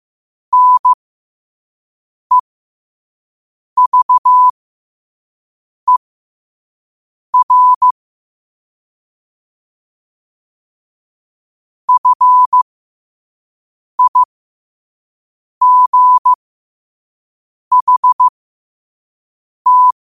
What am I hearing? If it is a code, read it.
Morse code, heard as a keyed tone, about 15 wpm, spaced as for 5 wpm, 1000 Hz. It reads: NEVER FIGHT
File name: qotdfiles / 5wpm